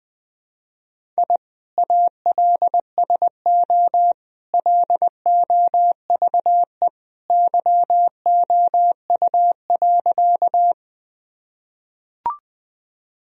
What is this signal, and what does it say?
Morse code, heard as a keyed tone, about 19 wpm, 700 Hz. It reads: I ALSO LOVE YOU.
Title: Sentences from Top 1000 Words - Review Part 1 - 20wpm